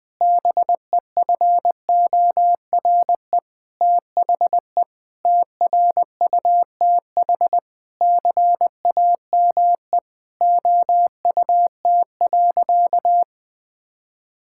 Morse code, heard as a keyed tone, about 20 wpm, 700 Hz. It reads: BEFORE THE TRUTH CAME OUT.
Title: Sentences with 9-Word Phrases - 20wpm